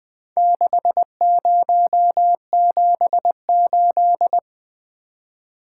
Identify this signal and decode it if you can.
Morse code, heard as a keyed tone, about 22 wpm, 700 Hz. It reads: 6078
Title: Speed-Racing - Four Letter-Number Combinations - 20wpm